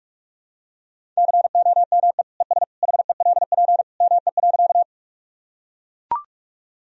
Morse code, heard as a keyed tone, about 42 wpm, 700 Hz. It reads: COME ES HELP ME.
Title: Sentences from Top 200 Words and Abbreviations 45wpm